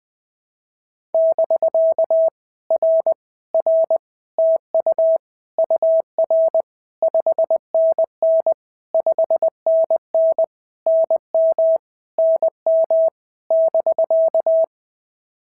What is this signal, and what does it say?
Morse code, heard as a keyed tone, about 20 wpm, 650 Hz. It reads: <BK> R R TU UR 5NN 5NN NM NM <BK>
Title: POTA QSO Practice - 20wpm